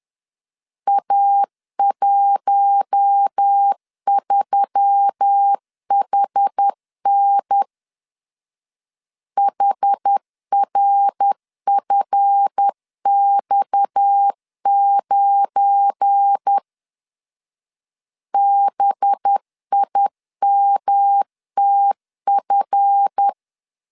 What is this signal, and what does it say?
Morse code, heard as a keyed tone, about 10 wpm, 800 Hz. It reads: A13HN HRFX9 BIMTF
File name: LEZIONI / 7 GF9Z / 7 GF9 10wpm 2747